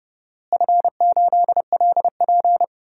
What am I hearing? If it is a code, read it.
Morse code, heard as a keyed tone, about 30 wpm, 700 Hz. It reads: F8LP
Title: DXCC-100 Call Signs - 4 Characters - Repeated 3x - 30wpm